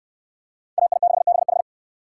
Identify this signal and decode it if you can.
Morse code, heard as a keyed tone, about 69 wpm, 700 Hz. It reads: AI6BH